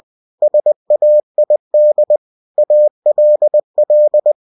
Morse code, heard as a keyed tone, about 20 wpm, 600 Hz. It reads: SAID ALL
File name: cw-books / winnie-the-pooh / winnie-20wpm-page-0105